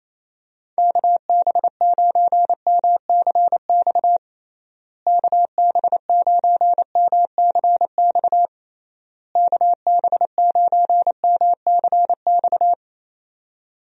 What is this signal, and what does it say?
Morse code, heard as a keyed tone, about 28 wpm, 700 Hz. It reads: KB9MCX KB9MCX KB9MCX
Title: USA Call Signs - 6 Characters - Repeated 3x - 28wpm